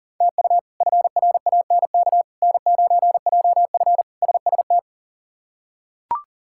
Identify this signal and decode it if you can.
Morse code, heard as a keyed tone, about 40 wpm, 700 Hz. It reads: TU FRANK N9JF SST
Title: K1USN SST QSO Practice - 40wpm